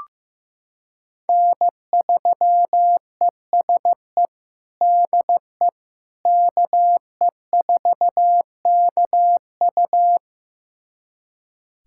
Morse code, heard as a keyed tone, about 16 wpm, 700 Hz. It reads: N3ESE DE KE4KU
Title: USA Callsign DE Callsign - 5 Characters - 15wpm